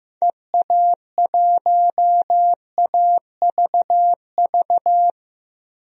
Morse code, heard as a keyed tone, about 15 wpm, 700 Hz. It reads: EA1AVV
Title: DXCC-100 Call Signs - 6 Characters - Repeated 3x - 15wpm